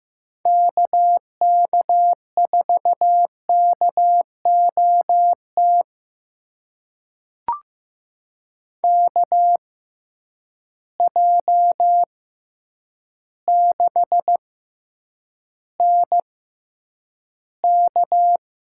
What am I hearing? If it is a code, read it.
Morse code, heard as a keyed tone, about 15 wpm, spaced as for 8 wpm, 700 Hz. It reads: KK4KOT KJ6NK
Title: USA Call Signs - 6 Characters - 6x Character Spacing - 15wpm